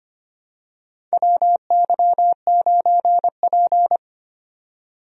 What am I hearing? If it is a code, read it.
Morse code, heard as a keyed tone, about 25 wpm, 700 Hz. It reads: WY9P
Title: USA Call Signs - 4 Characters - 25wpm